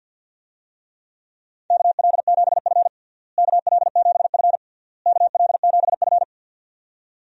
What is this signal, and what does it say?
Morse code, heard as a keyed tone, about 48 wpm, 700 Hz. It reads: KL7F KL7F KL7F
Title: DXCC-100 Call Signs - 4 Characters - Repeated 3x - 50wpm